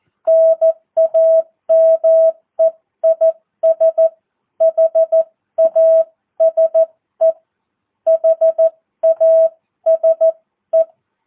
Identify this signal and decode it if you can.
Morse code, heard as a keyed tone, about 14 wpm, spaced as for 12 wpm, 650 Hz. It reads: NAME IS HASE HASE
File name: jh8cbh / 2022cw